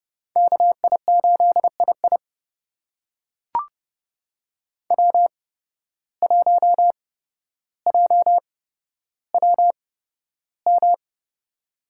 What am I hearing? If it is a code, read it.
Morse code, heard as a keyed tone, about 30 wpm, spaced as for 13 wpm, 700 Hz. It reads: KI8II W1JWM
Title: USA Call Signs - 8x Character Spacing - 30wpm